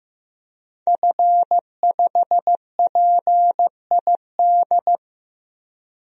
Morse code, heard as a keyed tone, about 15 wpm, 700 Hz. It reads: F5PID
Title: DXCC-100 Call Signs - 5 Characters - 15wpm